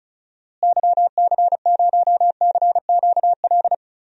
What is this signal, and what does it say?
Morse code, heard as a keyed tone, about 34 wpm, 700 Hz. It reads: YC0CQL
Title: DXCC-100 Call Signs - 6 Characters - Repeated 3x - 35wpm